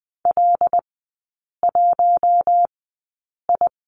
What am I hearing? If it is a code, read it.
Morse code, heard as a keyed tone, about 20 wpm, spaced as for 8 wpm, 700 Hz. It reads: L1I
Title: 074 Lesson 13e - Letter L - Three Characters 20wpm